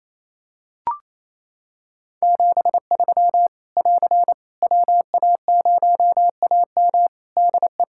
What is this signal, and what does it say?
Morse code, heard as a keyed tone, about 27 wpm, 700 Hz. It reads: 73 <AR> WA0AM DE
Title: K1USN SST QSO Practice 28wpm